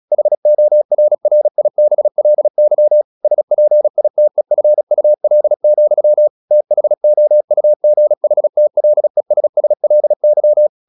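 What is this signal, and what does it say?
Morse code, heard as a keyed tone, about 36 wpm, 600 Hz. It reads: HORRIBLY SPITEFUL, THOUGHTLESSLY
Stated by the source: Morse practice at 36 wpm, text of Ken Follett's novel Never, from CW med Gnister.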